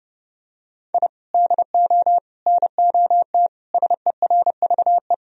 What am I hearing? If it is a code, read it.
Morse code, heard as a keyed tone, about 30 wpm, 700 Hz. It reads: I DO NOT SERVE
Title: Sentences from Top 1000 Words - Review Part 2 - 30wpm